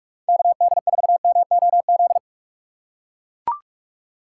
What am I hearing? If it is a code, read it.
Morse code, heard as a keyed tone, about 44 wpm, 700 Hz. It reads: KD4MOZ
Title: USA Call Signs - 6 Characters - 4x Character Spacing - 45wpm